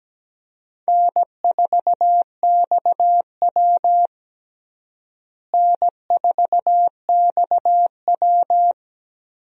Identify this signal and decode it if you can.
Morse code, heard as a keyed tone, about 17 wpm, 700 Hz. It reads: N4XW N4XW
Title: USA Call Signs - 4 Characters - Repeated 2x - 17wpm